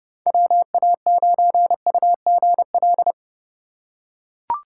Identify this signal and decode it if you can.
Morse code, heard as a keyed tone, about 30 wpm, 700 Hz. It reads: WA9UGL
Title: USA Call Signs - 6x Character Spacing - 30wpm